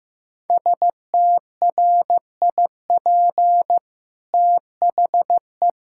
Morse code, heard as a keyed tone, about 16 wpm, 700 Hz. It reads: STRIP THE
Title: Sentences from Top 1600 Words 15wpm